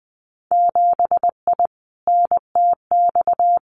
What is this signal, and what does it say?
Morse code, heard as a keyed tone, about 20 wpm, 700 Hz. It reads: 7I NTX
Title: Winter Field Day Exchanges Encore 20wpm